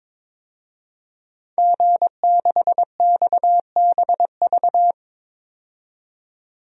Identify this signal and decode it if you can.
Morse code, heard as a keyed tone, about 23 wpm, 700 Hz. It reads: G6XBV
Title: DXCC-100 Call Signs - 5 Characters - 22wpm